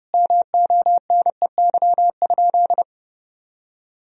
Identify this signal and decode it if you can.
Morse code, heard as a keyed tone, about 32 wpm, 700 Hz. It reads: MONEY?
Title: Sentences from Top 1000 Words - Review Part 4 - 30wpm